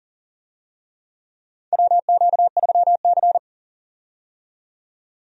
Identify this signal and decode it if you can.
Morse code, heard as a keyed tone, about 40 wpm, 700 Hz. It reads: WQ3C
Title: USA Call Signs - 4 Characters - 40wpm